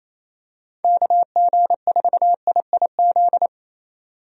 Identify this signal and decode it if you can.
Morse code, heard as a keyed tone, about 28 wpm, 700 Hz. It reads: KG4IIZ
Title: USA Call Signs - 6 Characters - 28wpm